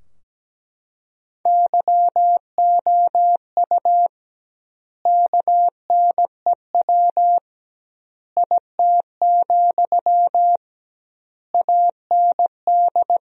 Morse code, heard as a keyed tone, about 17 wpm, 700 Hz. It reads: YOU KNEW IT, AND
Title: 2x Word-Spacing - Sentences from Top 500 Words - 17wpm